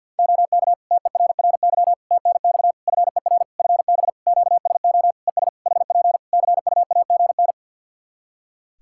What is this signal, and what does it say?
Morse code, heard as a keyed tone, about 48 wpm, 700 Hz. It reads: OK TERRY TNX FER FB QSO ES HP CUAGN
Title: Practice QSO - Exchange 4 - Ending - 50wpm